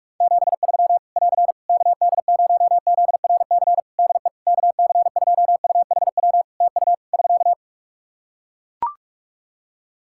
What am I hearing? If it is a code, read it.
Morse code, heard as a keyed tone, about 41 wpm, 700 Hz. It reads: 73 <AR> KD0ZRC DE KC2UHW TU <SK>
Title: Practice QSO - Exchange 4 - Ending - 45wpm